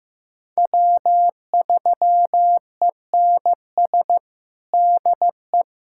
Morse code, heard as a keyed tone, about 16 wpm, 700 Hz. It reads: W3ENS DE
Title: USA Callsign DE Callsign - 5 Characters - 15wpm